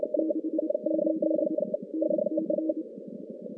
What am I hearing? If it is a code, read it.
Morse code, heard as a keyed tone, about 26 wpm, 320 Hz, with noise. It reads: 68 NC
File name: contests / CW-SS-k5tr- -2100